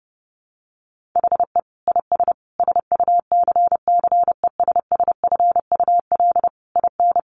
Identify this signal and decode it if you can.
Morse code, heard as a keyed tone, about 30 wpm, 700 Hz. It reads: HE IS SUCCESSFUL IN
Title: Sentences from Top 900 Words 30wpm